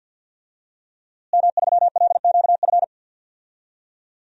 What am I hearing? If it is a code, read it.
Morse code, heard as a keyed tone, about 47 wpm, 700 Hz. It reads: M3LQF
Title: DXCC-100 Call Signs - 5 Characters - Repeated 2x - 50wpm